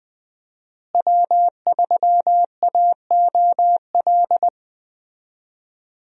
Morse code, heard as a keyed tone, about 21 wpm, 700 Hz. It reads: W3AOL
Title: USA Call Signs - 5 Characters - Repeated 2x - 20wpm